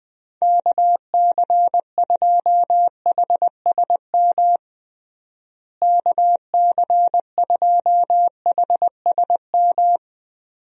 Morse code, heard as a keyed tone, about 20 wpm, 700 Hz. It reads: KC2HSM KC2HSM
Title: USA Call Signs - 6 Characters - Repeated 2x - 20wpm